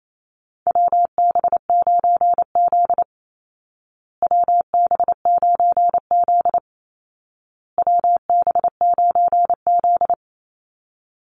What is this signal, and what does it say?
Morse code, heard as a keyed tone, about 28 wpm, 700 Hz. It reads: WB9Z WB9Z WB9Z
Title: Call Signs Repeated 2x - 28wpm